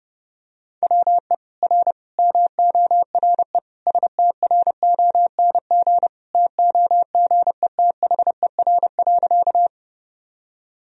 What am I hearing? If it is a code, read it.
Morse code, heard as a keyed tone, about 30 wpm, 700 Hz. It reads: WE R MORE STRONG TOGETHER.
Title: Sentences from Top 500 Words and Abbreviations 30wpm